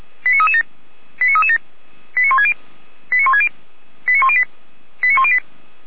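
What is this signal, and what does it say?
Five-tone selective call (ZVEI), heard as quick runs of five tones, tones of about 70 ms. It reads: 89337 89337 89160 89160 89108 89108
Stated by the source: ZVEI five-tone selective call received by SDR.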